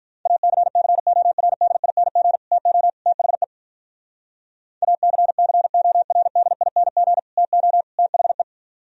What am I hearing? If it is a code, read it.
Morse code, heard as a keyed tone, about 51 wpm, 700 Hz. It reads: ACCORDING TO THE ACCORDING TO THE